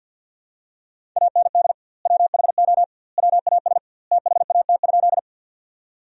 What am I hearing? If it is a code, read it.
Morse code, heard as a keyed tone, about 50 wpm, 700 Hz. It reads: AND WHO WAS THAT?
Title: 2x Word-Spacing - Sentences from Top 100 Words - 50wpm